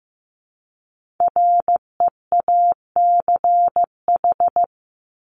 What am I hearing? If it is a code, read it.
Morse code, heard as a keyed tone, about 15 wpm, 700 Hz. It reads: REACH
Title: Top 500 Words Repeated 15wpm